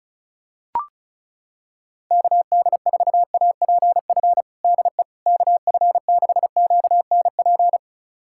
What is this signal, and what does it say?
Morse code, heard as a keyed tone, about 34 wpm, 700 Hz. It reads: KD4APF DE KF6QNP
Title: USA Callsign DE Callsign - 6 Characters - 35wpm